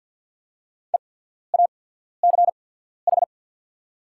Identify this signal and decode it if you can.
Morse code, heard as a keyed tone, about 48 wpm, spaced as for 14 wpm, 690 Hz. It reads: EACH
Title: 8x Character Spacing - Top 200 Words - 50wpm